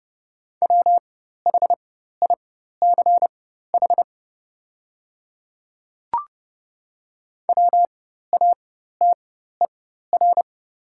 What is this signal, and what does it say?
Morse code, heard as a keyed tone, about 30 wpm, spaced as for 14 wpm, 700 Hz. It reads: WHICH WATER
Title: 4x Character Spacing - Top 200 Words - 30wpm